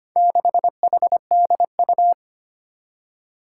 Morse code, heard as a keyed tone, about 25 wpm, 700 Hz. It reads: 6HDU
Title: Four Challenging Letter-Number Combinations 25wpm